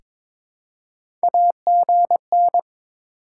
Morse code, heard as a keyed tone, about 22 wpm, 700 Hz. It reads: AGN
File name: Top 20 QSO Elements - 2x Character Spacing - 22wpm